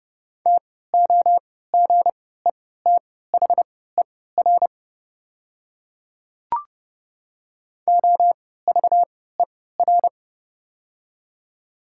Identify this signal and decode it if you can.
Morse code, heard as a keyed tone, about 30 wpm, spaced as for 17 wpm, 700 Hz. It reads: TOGETHER OVER
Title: 3x Character Spacing - Top 300 Words - 30wpm